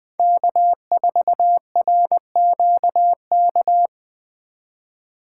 Morse code, heard as a keyed tone, about 20 wpm, 700 Hz. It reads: K4RQK
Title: Speed Racing - USA 1x3 Call Signs 17wpm